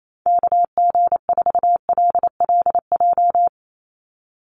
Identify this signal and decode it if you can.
Morse code, heard as a keyed tone, about 28 wpm, 700 Hz. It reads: KG4LLJ
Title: USA Call Signs 28wpm